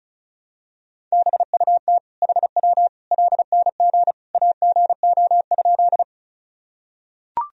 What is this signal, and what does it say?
Morse code, heard as a keyed tone, about 34 wpm, 700 Hz. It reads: BUT HW LNG AGO?
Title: Sentences from Top 500 Words and Abbreviations 35wpm